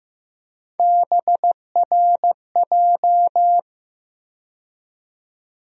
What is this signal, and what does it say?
Morse code, heard as a keyed tone, about 15 wpm, 700 Hz. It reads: BRJ
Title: USA Call Sign Postfixes - 3 Letters 15wpm